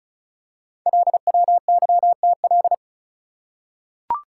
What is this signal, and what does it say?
Morse code, heard as a keyed tone, about 36 wpm, 690 Hz. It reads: LWYTL
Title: Code Groups - 5 Letters - 35wpm